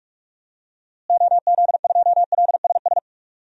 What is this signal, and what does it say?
Morse code, heard as a keyed tone, about 48 wpm, 700 Hz. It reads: OZ2LSS
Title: DXCC-100 Call Signs - 6 Characters - Repeated 3x - 45wpm